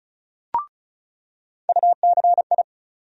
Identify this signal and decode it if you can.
Morse code, heard as a keyed tone, about 36 wpm, 700 Hz. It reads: UCI
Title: USA Call Sign Postfixes - 3 Letters 35wpm